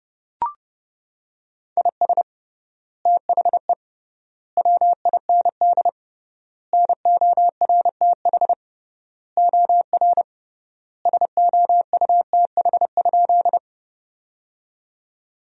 Morse code, heard as a keyed tone, about 30 wpm, 700 Hz. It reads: IS THE WIND NORTH OR SOUTH?
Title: 3x Word-Spacing - Sentences from Top 400 Words - 30wpm